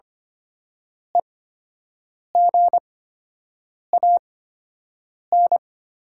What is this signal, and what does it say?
Morse code, heard as a keyed tone, about 25 wpm, spaced as for 7 wpm, 690 Hz. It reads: EGAN
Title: 8x Character Spacing - Top 500 Words - 25wpm